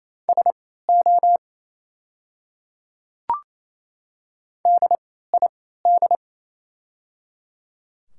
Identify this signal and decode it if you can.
Morse code, heard as a keyed tone, about 28 wpm, spaced as for 16 wpm, 700 Hz. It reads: SO DID
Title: 3x Character Spacing - Top 300 Words - 28wpm